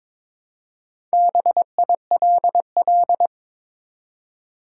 Morse code, heard as a keyed tone, about 22 wpm, 700 Hz. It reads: BILL
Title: Four Letter Words - 22wpm